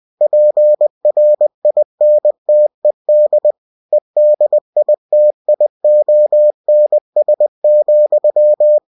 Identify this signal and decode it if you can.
Morse code, heard as a keyed tone, about 20 wpm, 600 Hz. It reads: PRINTED EDITIONS,
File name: cw-books / winnie-the-pooh / winnie-20wpm-page-0122